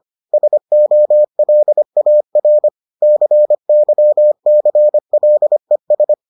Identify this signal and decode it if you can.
Morse code, heard as a keyed tone, about 25 wpm, 600 Hz. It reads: SOLAR CYCLES